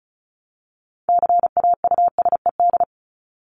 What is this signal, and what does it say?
Morse code, heard as a keyed tone, about 34 wpm, 700 Hz. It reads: CAUSED
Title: ED Suffix - Top 100 Words with Base Words 35wpm